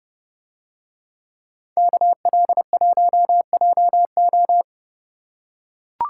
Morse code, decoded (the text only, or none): KL1JO